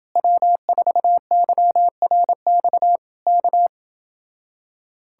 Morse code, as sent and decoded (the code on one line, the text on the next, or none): .-- ....- -.-- .-. -..- / -.-
W4YRX K